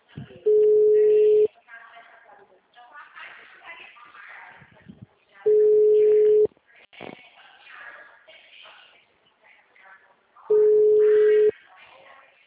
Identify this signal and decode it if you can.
Ringback tone (Europe, 425 Hz), with speech.